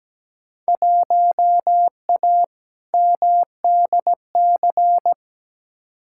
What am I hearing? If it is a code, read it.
Morse code, heard as a keyed tone, about 17 wpm, 700 Hz. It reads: 1A MDC
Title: Field Day Exchanges - 17wpm